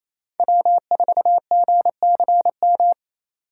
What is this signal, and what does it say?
Morse code, heard as a keyed tone, about 28 wpm, 700 Hz. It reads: W4GCM
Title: USA Call Signs - 4x Character Spacing - 28wpm